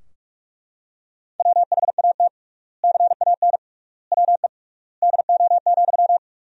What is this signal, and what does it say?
Morse code, heard as a keyed tone, about 42 wpm, 690 Hz. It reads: WHAT CAN WE DO,
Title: 3x Word-Spacing - Sentences from Top 100 Words - 45wpm